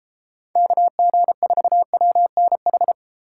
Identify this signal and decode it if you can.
Morse code, heard as a keyed tone, about 36 wpm, 700 Hz. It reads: KG4WNH